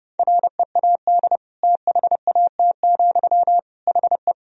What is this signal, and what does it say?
Morse code, heard as a keyed tone, about 30 wpm, 700 Hz. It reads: READ THAT, HE
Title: Sentences from Top 200 Words and Abbreviations 30wpm